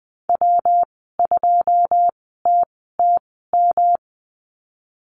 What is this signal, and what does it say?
Morse code, heard as a keyed tone, about 19 wpm, spaced as for 14 wpm, 700 Hz. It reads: W2TTM
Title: 154 Lesson 25g - Letter p - Call Signs 20wpm